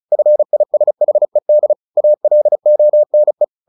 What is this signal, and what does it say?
Morse code, heard as a keyed tone, about 35 wpm, 600 Hz. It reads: FISHED ALONE
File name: the old man and the sea / the old man and the sea 1 35wpm